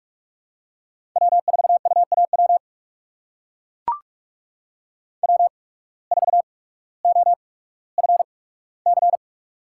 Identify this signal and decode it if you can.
Morse code, heard as a keyed tone, about 43 wpm, 690 Hz. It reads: W 4 U A W W 4 O F C
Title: USA Call Signs - 8x Character Spacing - 45wpm